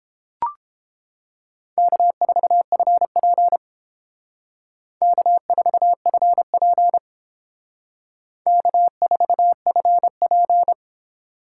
Morse code, heard as a keyed tone, about 28 wpm, 700 Hz. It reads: K4FP K4FP K4FP